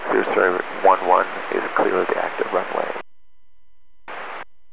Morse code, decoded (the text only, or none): none